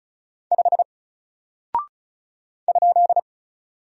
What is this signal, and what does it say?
Morse code, heard as a keyed tone, about 36 wpm, 700 Hz. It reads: 5 ?
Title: Single Character (Letter:Number:Punctuation:Prosign) Alt 35wpm